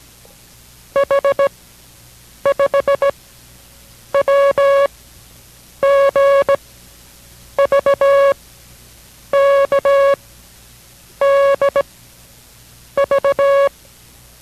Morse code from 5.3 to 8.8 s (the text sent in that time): G V